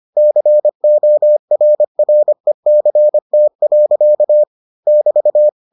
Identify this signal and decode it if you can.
Morse code, heard as a keyed tone, about 25 wpm, 600 Hz. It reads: CORRECT. <BT>